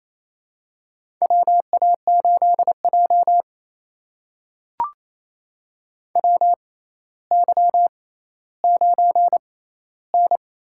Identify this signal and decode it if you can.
Morse code, heard as a keyed tone, about 28 wpm, spaced as for 14 wpm, 700 Hz. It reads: WA8J WY9N
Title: USA Call Signs - 6x Character Spacing - 28wpm